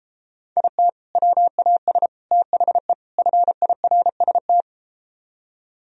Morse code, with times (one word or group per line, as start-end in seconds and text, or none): 0.6-0.9 IT
1.2-2.1 WAS
2.3-2.9 THE
3.2-4.6 FIRST